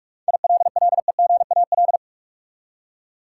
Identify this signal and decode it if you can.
Morse code, heard as a keyed tone, about 47 wpm, 700 Hz. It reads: ILLEGAL
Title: Seven Letter Words - 45wpm